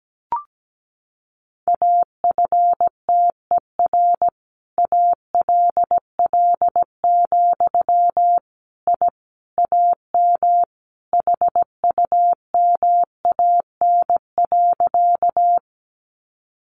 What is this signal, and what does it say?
Morse code, heard as a keyed tone, about 17 wpm, 700 Hz. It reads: AFTER ALL, I AM HUMAN.
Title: Sentences from Top 600 Words 17wpm